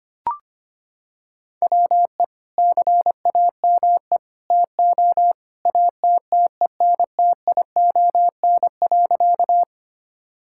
Morse code, heard as a keyed tone, about 25 wpm, 700 Hz. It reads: WE CAME TO ATTENTION.
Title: Sentences from Top 1000 Words - Review Part 6 - 25wpm